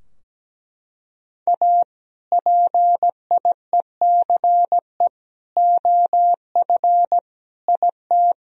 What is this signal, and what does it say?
Morse code, heard as a keyed tone, about 17 wpm, 700 Hz. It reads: A PIECE OF IT
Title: Sentences from Top 1000 Words - Review Part 2 - 17wpm